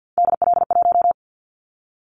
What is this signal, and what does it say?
Morse code, heard as a keyed tone, about 50 wpm, 700 Hz. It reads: DL1
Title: Callsign Prefixes for Top 25 DXCC - 50wpm